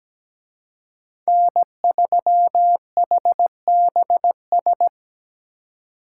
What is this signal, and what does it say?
Morse code, heard as a keyed tone, about 20 wpm, 700 Hz. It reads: N3HBS
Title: Speed Racing - USA 1x3 Call Signs 17wpm